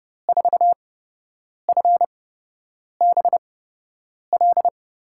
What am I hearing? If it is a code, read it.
Morse code, heard as a keyed tone, about 32 wpm, spaced as for 8 wpm, 700 Hz. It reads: 4FBL
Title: USA Call Signs - 6 Characters - 8x Character Spacing - 30wpm